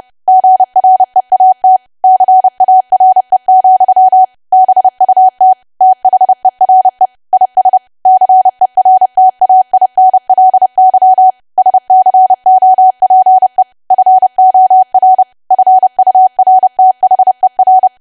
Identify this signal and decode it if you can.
Morse code, heard as a keyed tone, about 30 wpm, 750 Hz. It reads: GREAT CARE, BUT THERE IS CERTAINLY SCOPE FOR FURTHER
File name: cwa / W1AW-Level3 / 30WPM